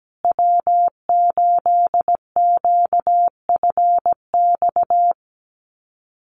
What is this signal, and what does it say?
Morse code, heard as a keyed tone, about 17 wpm, 700 Hz. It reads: W8QFX
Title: USA Call Signs Repeated 2x 17wpm